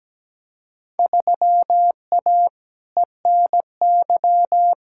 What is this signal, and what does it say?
Morse code, heard as a keyed tone, about 17 wpm, 700 Hz. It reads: 3A ENY
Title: Field Day Exchanges - 17wpm